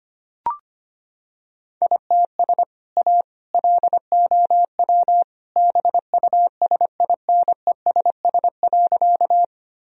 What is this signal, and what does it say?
Morse code, heard as a keyed tone, about 25 wpm, 700 Hz. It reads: ITS A LOW BUSINESS.
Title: Sentences from Top 1000 Words - Review Part 1 - 25wpm